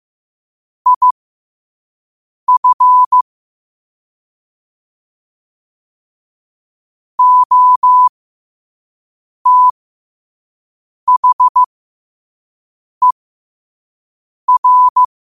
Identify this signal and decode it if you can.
Morse code, heard as a keyed tone, about 15 wpm, spaced as for 5 wpm, 1000 Hz. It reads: IF OTHER
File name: qotdfiles / 5wpm